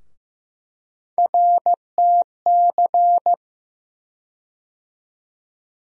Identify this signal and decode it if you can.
Morse code, heard as a keyed tone, about 15 wpm, 700 Hz. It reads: RTC
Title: USA Call Sign Postfixes - 3 Letters 15wpm